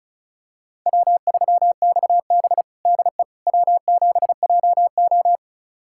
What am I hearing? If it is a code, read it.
Morse code, heard as a keyed tone, about 34 wpm, 700 Hz. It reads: W3XB DE W7JO
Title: USA Callsign DE Callsign - 4 Characters - 35wpm